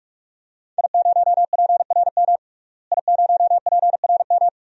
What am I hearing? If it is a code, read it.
Morse code, heard as a keyed tone, about 46 wpm, 700 Hz. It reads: I0PRM I0PRM
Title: DXCC-100 Call Signs - 5 Characters - Repeated 3x - 45wpm